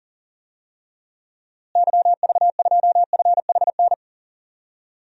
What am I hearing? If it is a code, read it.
Morse code, heard as a keyed tone, about 40 wpm, 700 Hz. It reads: YV2FHN
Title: DXCC-100 Call Signs - 6 Characters - Repeated 3x - 40wpm